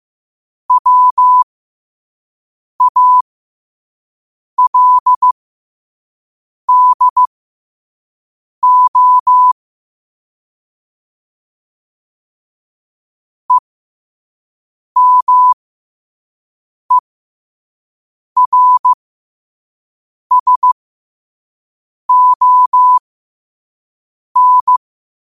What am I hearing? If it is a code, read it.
Morse code, heard as a keyed tone, about 15 wpm, spaced as for 5 wpm, 1000 Hz. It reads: WALDO EMERSON
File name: qotdfiles / 5wpm